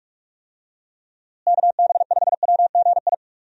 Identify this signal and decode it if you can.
Morse code, heard as a keyed tone, about 45 wpm, 700 Hz. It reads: KB5WGI
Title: USA Call Signs - 6 Characters - Repeated 3x - 45wpm